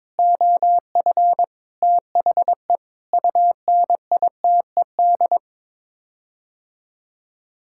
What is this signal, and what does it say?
Morse code, heard as a keyed tone, about 22 wpm, 700 Hz. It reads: OF THE UNITED
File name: practice / 22wpm / Sets of 8 Words - 22wpm